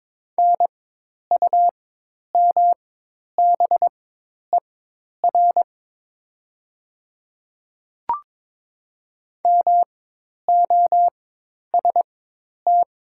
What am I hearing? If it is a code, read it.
Morse code, heard as a keyed tone, about 22 wpm, spaced as for 10 wpm, 700 Hz. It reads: NUMBER MOST
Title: 4x Character Spacing - Top 200 Words - 22wpm